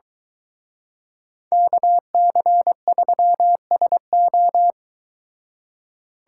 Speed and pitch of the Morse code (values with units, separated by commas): 23 wpm, 700 Hz